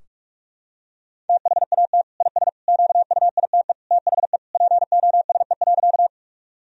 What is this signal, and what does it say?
Morse code, heard as a keyed tone, about 42 wpm, 700 Hz. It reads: THAT IS QUITE THE POSE.
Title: Sentences from Top 1000 Words - Review Part 2 - 45wpm